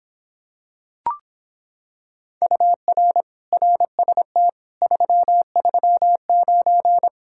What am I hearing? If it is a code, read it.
Morse code, heard as a keyed tone, about 26 wpm, 700 Hz. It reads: UR RST 339